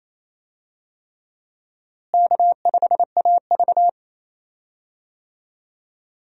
Morse code, heard as a keyed tone, about 28 wpm, 700 Hz. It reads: K5AV
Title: USA Call Signs - 4 Characters - Repeated 3x - 28wpm